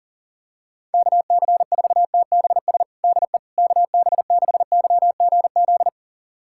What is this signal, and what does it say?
Morse code, heard as a keyed tone, about 40 wpm, 700 Hz. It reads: KC4TBS DE KB6YGZ
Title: USA Callsign DE Callsign - 6 Characters - 40wpm